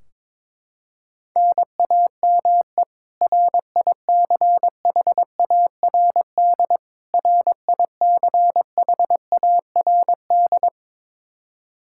Morse code, heard as a keyed tone, about 22 wpm, 700 Hz. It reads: NAME RICHARD RICHARD
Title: Practice QSO - Exchange 1 - Beginning - 22wpm